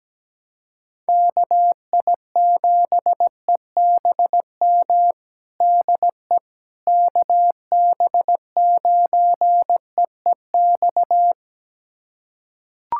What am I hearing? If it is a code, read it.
Morse code, heard as a keyed tone, about 17 wpm, 700 Hz. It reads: KI7EBM DE KB9EEX
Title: USA Callsign DE Callsign - 6 Characters - 17wpm